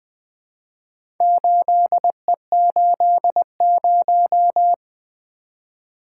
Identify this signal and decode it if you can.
Morse code, heard as a keyed tone, about 24 wpm, 700 Hz. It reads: 8E80
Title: Speed-Racing - Four Letter-Number Combinations - 20wpm